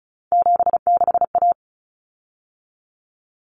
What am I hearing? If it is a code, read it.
Morse code, heard as a keyed tone, about 34 wpm, 700 Hz. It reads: 76A
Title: Common Cut Numbers - 3-Digit 35wpm